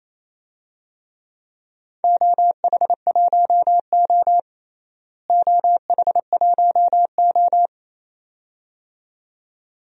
Morse code, heard as a keyed tone, about 28 wpm, 700 Hz. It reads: OH1O OH1O
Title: DXCC-100 Call Signs - 4 Characters - Repeated 2x - 28wpm